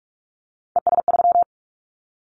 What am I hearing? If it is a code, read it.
Morse code, heard as a keyed tone, about 46 wpm, 700 Hz. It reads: ES3
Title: Callsign Prefixes for Top 50 DXCC - 45wpm